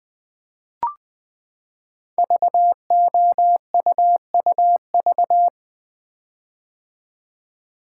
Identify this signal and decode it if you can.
Morse code, heard as a keyed tone, about 20 wpm, 700 Hz. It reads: VOUUV
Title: Code Groups - 5 Letters - 20wpm